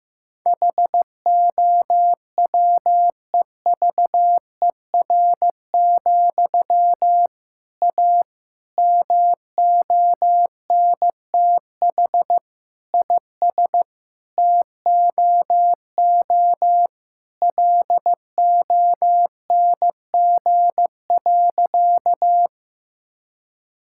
Morse code, heard as a keyed tone, about 15 wpm, 700 Hz. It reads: HOWEVER, A MONTH IS TOO LONG.
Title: Sentences from Top 1000 Words - Review Part 5 - 15wpm